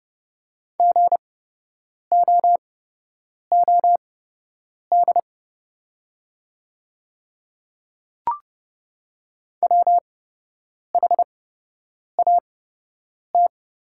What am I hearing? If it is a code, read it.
Morse code, heard as a keyed tone, about 29 wpm, spaced as for 8 wpm, 700 Hz. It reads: GOOD WHAT
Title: 8x Character Spacing - Top 200 Words - 30wpm